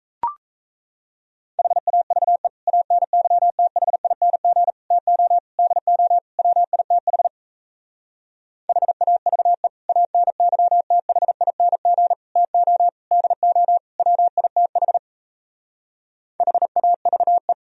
Morse code, heard as a keyed tone, about 35 wpm, 700 Hz. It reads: HAVE ANYTHING TO DO WITH HAVE ANYTHING TO DO WITH HAVE
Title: Speed-Racing - Sets of 5 Words 28wpm